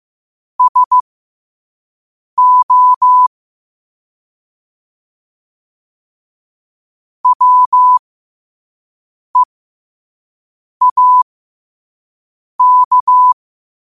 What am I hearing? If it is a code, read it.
Morse code, heard as a keyed tone, about 15 wpm, spaced as for 5 wpm, 1000 Hz. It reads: SO WEAK